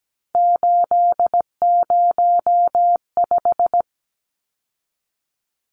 Morse code, heard as a keyed tone, about 17 wpm, 700 Hz. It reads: 805
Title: Numbers - 3-Digit 17wpm